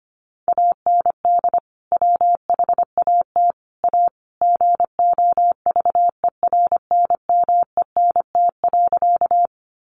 Morse code, heard as a keyed tone, about 25 wpm, 700 Hz. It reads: AND WHAT A GOVERNMENT.
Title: Sentences from Top 600 Words 25wpm